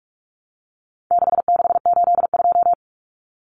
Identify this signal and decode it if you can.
Morse code, heard as a keyed tone, about 44 wpm, 700 Hz. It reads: 6682
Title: Numbers - 4-Digit 45wpm